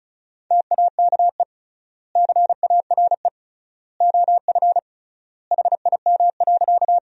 Morse code, heard as a keyed tone, about 36 wpm, 700 Hz. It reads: TAKE CARE OF HIM.
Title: 3x Word-Spacing - Sentences from Top 300 Words - 35wpm